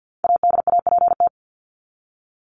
Morse code, heard as a keyed tone, about 48 wpm, 700 Hz. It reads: ADAPT
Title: Top 1501-1750 Words 50wpm